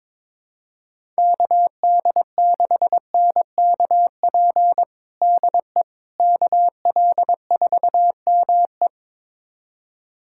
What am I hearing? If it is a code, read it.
Morse code, heard as a keyed tone, about 21 wpm, 700 Hz. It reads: KD6NKP DE KL4ME
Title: USA Callsign DE Callsign 22wpm